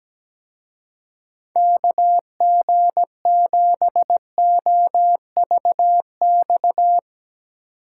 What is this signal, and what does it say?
Morse code, heard as a keyed tone, about 17 wpm, 700 Hz. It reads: KG7OVX
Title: Speed Racing - USA 2x3 Call Signs 17wpm